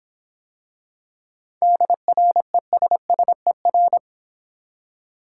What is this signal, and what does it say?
Morse code, heard as a keyed tone, about 23 wpm, 700 Hz. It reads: DRESSER
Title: Difficult Words - Speed-Racing with Hints - 20wpm